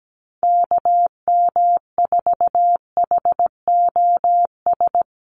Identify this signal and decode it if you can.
Morse code, heard as a keyed tone, about 17 wpm, 700 Hz. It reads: KM4HOS
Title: USA Call Signs Repeated 3x 17wpm